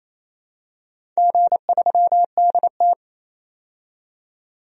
Morse code, heard as a keyed tone, about 28 wpm, 700 Hz. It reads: G3DT
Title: DXCC-100 Call Signs - 4 Characters - 28wpm